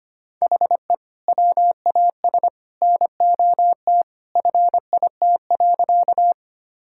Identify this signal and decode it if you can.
Morse code, heard as a keyed tone, about 25 wpm, 700 Hz. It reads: HE WAS NOT FIT.
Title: Sentences from Top 1000 Words - Review Part 5 - 25wpm